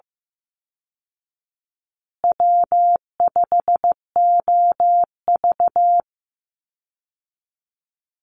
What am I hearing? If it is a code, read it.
Morse code, heard as a keyed tone, about 14 wpm, 700 Hz. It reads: W5OV
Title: Easy Call Signs Repeated 3x - 15wpm